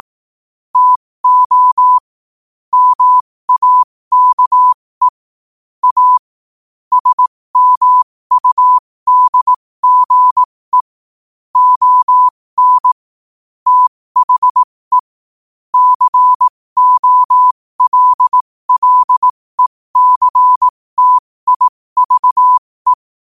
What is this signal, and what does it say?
Morse code, heard as a keyed tone, about 18 wpm, spaced as for 15 wpm, 1000 Hz. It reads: TO MAKE A SMUDGE ON THE COLLECTIVE